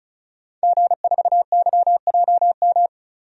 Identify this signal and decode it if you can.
Morse code, heard as a keyed tone, about 36 wpm, 700 Hz. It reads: G4YJM
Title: DXCC-100 Call Signs - 5 Characters - 35wpm